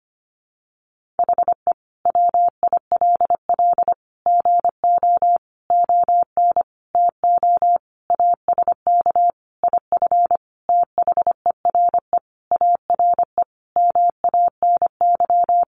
Morse code, heard as a keyed tone, about 25 wpm, 700 Hz. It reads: HE WILL GO ON TO ASK IF THERE ARE MANY
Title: Sentences from Top 1500 Words 25wpm